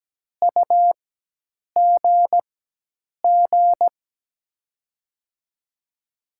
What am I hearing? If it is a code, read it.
Morse code, heard as a keyed tone, about 17 wpm, spaced as for 8 wpm, 700 Hz. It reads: UGG
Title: USA Call Signs - 6 Characters - 4x Character Spacing - 17wpm